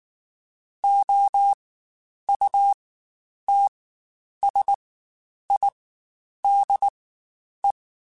Morse code, heard as a keyed tone, about 19 wpm, spaced as for 9 wpm, 800 Hz. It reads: OUTSIDE
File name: Spaced Characters / Spaced Characters-mp3s / Weather 4 14 sp @18